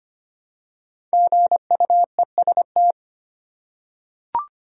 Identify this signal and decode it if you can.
Morse code, heard as a keyed tone, about 25 wpm, 700 Hz. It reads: GUEST
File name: 5 Letter Words - First 3 Letters Given - 25wpm